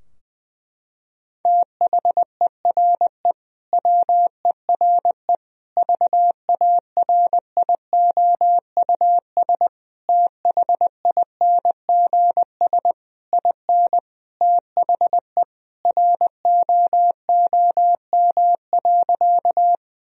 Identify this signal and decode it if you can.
Morse code, heard as a keyed tone, about 21 wpm, 700 Hz. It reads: THERE WERE VARIOUS THINGS IN THE ROOM.
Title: Sentences from Top 1000 Words - Review Part 4 - 20wpm